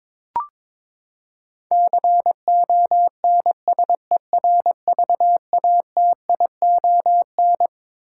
Morse code, heard as a keyed tone, about 22 wpm, 700 Hz. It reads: CONSERVATION
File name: Twelve Letter Words - 22wpm